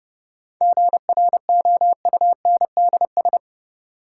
Morse code, heard as a keyed tone, about 30 wpm, 700 Hz. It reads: GROUNDS
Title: Sentences from Top 1300 Words - New Word Drill Alt - 30wpm